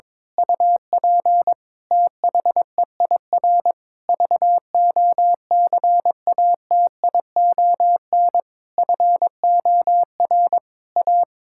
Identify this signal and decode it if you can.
Morse code, heard as a keyed tone, about 22 wpm, 700 Hz. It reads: UP THEIR VOCATION FOR A
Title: Pangram Challenge v2 - 22wpm